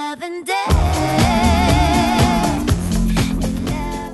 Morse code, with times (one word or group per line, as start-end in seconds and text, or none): none